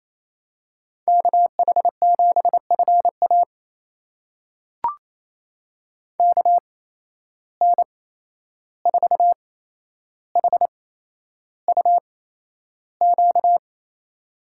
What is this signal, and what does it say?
Morse code, heard as a keyed tone, about 28 wpm, spaced as for 11 wpm, 700 Hz. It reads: KH7FA KN4HUQ
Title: USA Call Signs - 8x Character Spacing - 28wpm